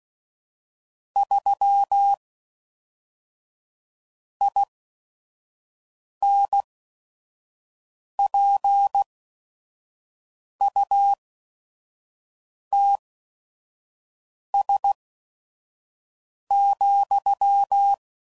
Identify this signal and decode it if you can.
Morse code, heard as a keyed tone, about 16 wpm, spaced as for 5 wpm, 800 Hz. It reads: 3 INPUTS,
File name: CW Code Groups @5WPM / Technical Subjects / AND Gate